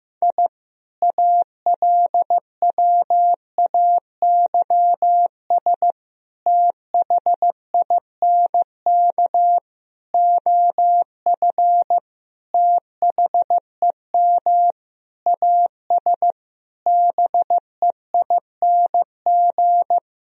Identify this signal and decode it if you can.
Morse code, heard as a keyed tone, about 15 wpm, 700 Hz. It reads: I ALWAYS THINK OF THEM AS BEING
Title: Sentences from Top 1000 Words - Review Part 3 - 15wpm